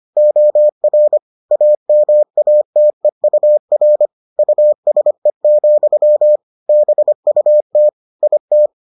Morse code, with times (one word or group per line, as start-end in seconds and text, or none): 0.2-1.2 OR
1.5-4.1 AMATEUR
4.4-6.4 USE,
6.7-7.9 BUT
8.2-8.7 IT